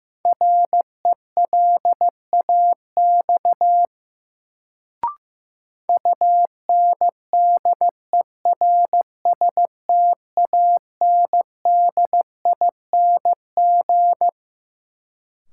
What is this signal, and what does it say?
Morse code, heard as a keyed tone, about 15 wpm, 700 Hz. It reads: RELAX UNDERSTANDING
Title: Sentences from Top 1400 Words - New Word Drill Alt - 15wpm